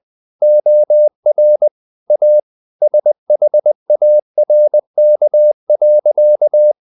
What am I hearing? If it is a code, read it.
Morse code, heard as a keyed tone, about 20 wpm, 600 Hz. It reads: OR A SHARK.
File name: the old man and the sea / the old man and the sea part 1 20wpm